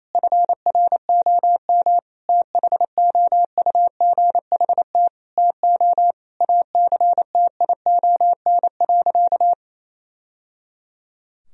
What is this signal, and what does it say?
Morse code, heard as a keyed tone, about 28 wpm, 700 Hz. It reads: FROM THOUGHT TO ACTION.
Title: Sentences from Top 1000 Words - Review Part 4 - 28wpm